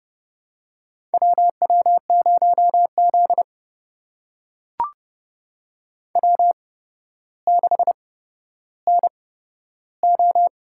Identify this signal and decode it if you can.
Morse code, heard as a keyed tone, about 29 wpm, spaced as for 13 wpm, 700 Hz. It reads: WW0Z W6NO
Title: USA Call Signs - 4 Characters - 8x Character Spacing - 30wpm